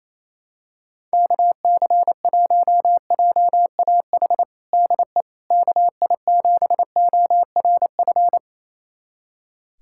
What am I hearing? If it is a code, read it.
Morse code, heard as a keyed tone, about 28 wpm, 700 Hz. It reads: KC1JAH DE KI7ORF
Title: USA Callsign DE Callsign 28wpm